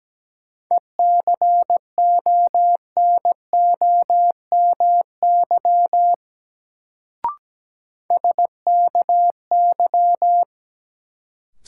Morse code, heard as a keyed tone, about 17 wpm, 700 Hz. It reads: ECONOMY SKY
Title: Sentences from Top 1200 Words - New Word Drill Alt - 17wpm